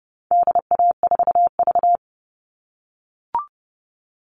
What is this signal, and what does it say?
Morse code, heard as a keyed tone, about 30 wpm, 700 Hz. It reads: DA4V
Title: All Cut Numbers - 4-Digit 30wpm